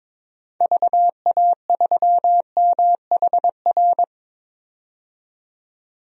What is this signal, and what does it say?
Morse code, heard as a keyed tone, about 22 wpm, 700 Hz. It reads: VA3MHR
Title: DXCC-100 Call Signs - 6 Characters - 22wpm